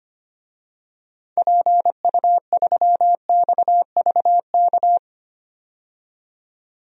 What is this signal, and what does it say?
Morse code, heard as a keyed tone, about 25 wpm, 700 Hz. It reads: PU3XVK
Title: DXCC-100 Call Signs - 6 Characters - 25wpm